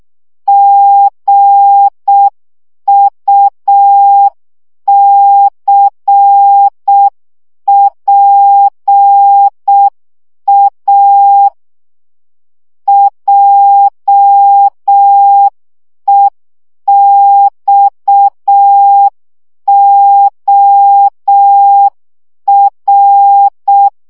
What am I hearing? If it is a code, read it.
Morse code, heard as a keyed tone, about 6 wpm, 800 Hz. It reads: GUCPA JEXOR
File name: morze vizsga 2011-1 5